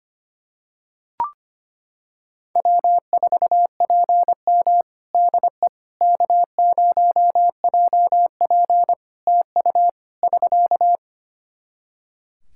Morse code, heard as a keyed tone, about 25 wpm, 700 Hz. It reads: W4PM DE K0JP TU <SK>
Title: Practice QSO - Exchange 4 - Ending - 25wpm